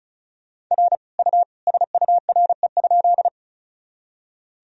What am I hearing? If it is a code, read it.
Morse code, heard as a keyed tone, about 36 wpm, 700 Hz. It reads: R U SURE?
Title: Sentences from Top 300 Words and Abbreviations 35wpm